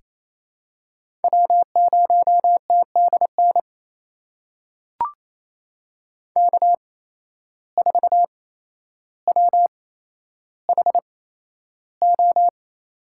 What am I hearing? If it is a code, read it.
Morse code, heard as a keyed tone, about 28 wpm, spaced as for 12 wpm, 700 Hz. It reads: W0TDN K4WHO
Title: USA Call Signs - 5 Characters - 8x Character Spacing - 28wpm